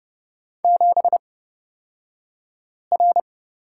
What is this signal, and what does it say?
Morse code, heard as a keyed tone, about 30 wpm, spaced as for 5 wpm, 700 Hz. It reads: 7R
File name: Lesson 36d - BT - Two Characters 30wpm